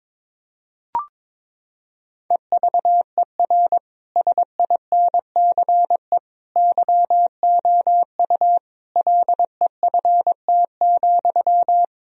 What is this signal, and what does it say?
Morse code, heard as a keyed tone, about 22 wpm, 700 Hz. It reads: EVER SINCE YOU LEFT,
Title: Sentences from Top 1000 Words - Review Part 1 - 22wpm